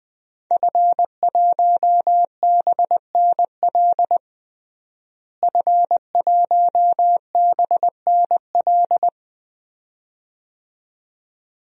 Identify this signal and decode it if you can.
Morse code, heard as a keyed tone, about 20 wpm, 700 Hz. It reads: F1BNL F1BNL
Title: DXCC-100 Call Signs - 5 Characters - Repeated 2x - 20wpm